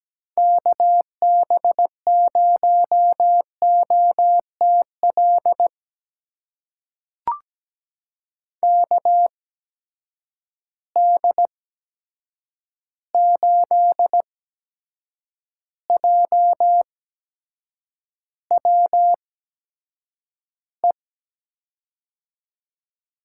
Morse code, heard as a keyed tone, about 17 wpm, spaced as for 7 wpm, 700 Hz. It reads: KB0OTL KD8JWE
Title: USA Call Signs - 6 Characters - 8x Character Spacing - 17wpm